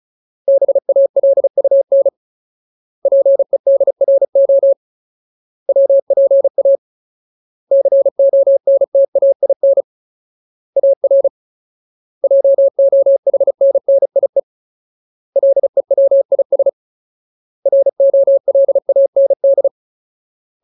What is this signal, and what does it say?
Morse code, heard as a keyed tone, about 34 wpm, 550 Hz. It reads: BALUN PEDRO WPA CONTAIN AR JOHNNIE LEWIS ROLAND
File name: Ultimate Word List / 35WPM Ultimate